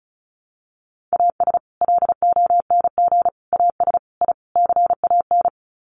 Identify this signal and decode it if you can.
Morse code, heard as a keyed tone, about 34 wpm, 700 Hz. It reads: AS LONG AS I CAN
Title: Sets of 5 Words Encore2 35wpm